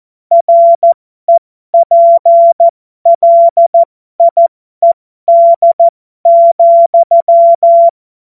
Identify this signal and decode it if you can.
Morse code, heard as a keyed tone, about 14 wpm, spaced as for 12 wpm, 680 Hz. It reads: REPLIED,
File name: Morsekurs / Witze